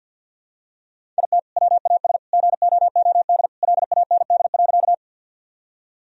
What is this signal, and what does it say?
Morse code, heard as a keyed tone, about 51 wpm, 700 Hz. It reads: IT WAS GOOD LAND.
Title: Sentences from Top 1000 Words - Review Part 1 - 50wpm